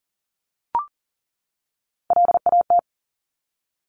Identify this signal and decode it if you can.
Morse code, heard as a keyed tone, about 41 wpm, 700 Hz. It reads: LAT
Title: Top Three Letter Combinations in Words 40wpm